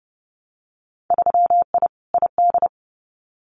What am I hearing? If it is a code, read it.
Morse code, heard as a keyed tone, about 30 wpm, 700 Hz. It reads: 3I ID
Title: Winter Field Day Exchanges 30wpm